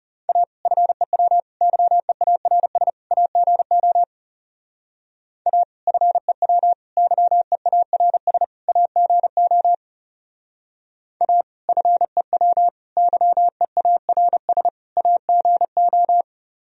Morse code, heard as a keyed tone, about 34 wpm, 700 Hz. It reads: A FEW YEARS AGO A FEW YEARS AGO A FEW YEARS AGO
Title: Speed-Racing - Sets of 4 Words 30wpm